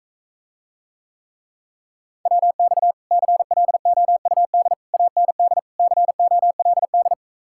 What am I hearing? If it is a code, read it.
Morse code, heard as a keyed tone, about 45 wpm, 700 Hz. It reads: WX CLOUD AND COLD